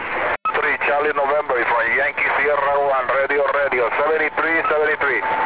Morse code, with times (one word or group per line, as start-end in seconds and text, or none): none